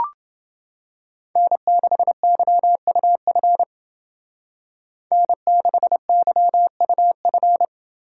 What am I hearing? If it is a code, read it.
Morse code, heard as a keyed tone, about 29 wpm, 700 Hz. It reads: N6YUF N6YUF